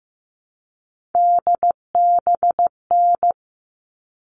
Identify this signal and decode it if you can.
Morse code, heard as a keyed tone, about 15 wpm, 700 Hz. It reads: DBN
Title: All Cut Numbers - 3-Digit 15wpm